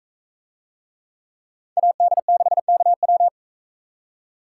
Morse code, heard as a keyed tone, about 42 wpm, 700 Hz. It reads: AD6KW